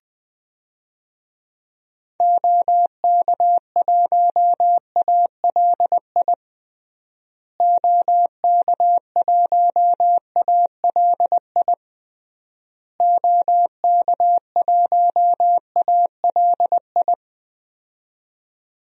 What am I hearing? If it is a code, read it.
Morse code, heard as a keyed tone, about 20 wpm, 700 Hz. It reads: OK1ALI OK1ALI OK1ALI
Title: DXCC-100 Call Signs - 6 Characters - Repeated 3x - 20wpm